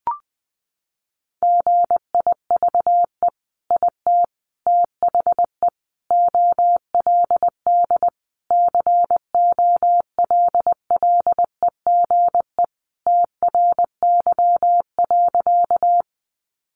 Morse code, heard as a keyed tone, about 20 wpm, 700 Hz. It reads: GIVE IT THE OLD COLLEGE TRY.
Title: Sentences from Top 1300 Words 20wpm